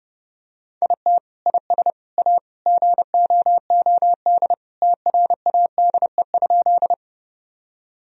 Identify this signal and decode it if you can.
Morse code, heard as a keyed tone, about 30 wpm, 700 Hz. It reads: IT IS A GOOD TRADE?
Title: Sentences from Top 1000 Words - Review Part 4 - 30wpm